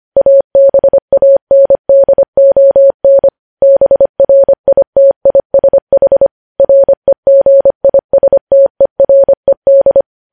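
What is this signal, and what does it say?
Morse code, heard as a keyed tone, about 25 wpm, 550 Hz. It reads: ABANDON BRITISH REGISTERED
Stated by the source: Morse code news headlines sent at 25 wpm.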